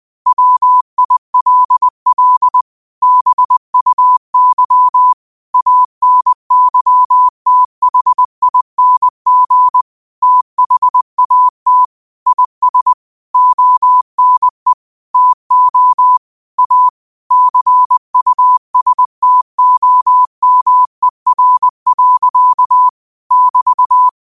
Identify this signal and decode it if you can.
Morse code, heard as a keyed tone, about 20 wpm, 1000 Hz. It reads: WILL BUY ANYTHING THAT IS ONE TO A CUSTOMER. <BT>